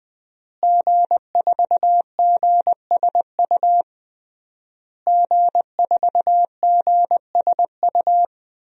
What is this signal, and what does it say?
Morse code, heard as a keyed tone, about 20 wpm, 700 Hz. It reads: G4GSU G4GSU
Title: DXCC-100 Call Signs - 5 Characters - Repeated 3x - 20wpm